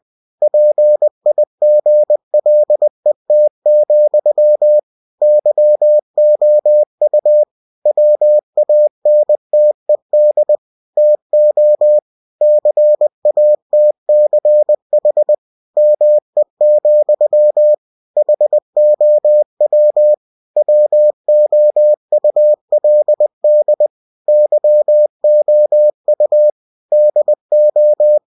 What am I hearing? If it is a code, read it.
Morse code, heard as a keyed tone, about 20 wpm, 600 Hz. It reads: PIGLET, YOU WANTED TO CATCH ME, HOW WOULD YOU DO
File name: cw-books / winnie-the-pooh / winnie-20wpm-page-0036